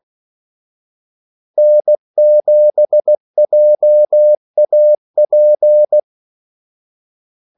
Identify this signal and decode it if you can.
Morse code, heard as a keyed tone, about 16 wpm, 600 Hz. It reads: N7JAP
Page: Kaldesignaler 16 wpm